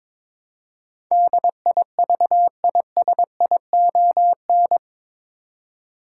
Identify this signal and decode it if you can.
Morse code, heard as a keyed tone, about 22 wpm, 700 Hz. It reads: DIVISION
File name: practice / 22wpm / Eight Letter Words - 22wpm